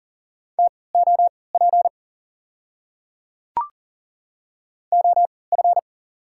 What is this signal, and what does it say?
Morse code, heard as a keyed tone, about 40 wpm, spaced as for 23 wpm, 690 Hz. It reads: TOP OF
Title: 3x Character Spacing - Top 500 Words - 40wpm